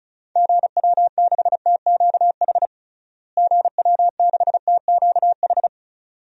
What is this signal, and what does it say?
Morse code, heard as a keyed tone, about 36 wpm, 700 Hz. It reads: GW6TQH GW6TQH
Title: DXCC-100 Call Signs - 6 Characters - Repeated 2x - 35wpm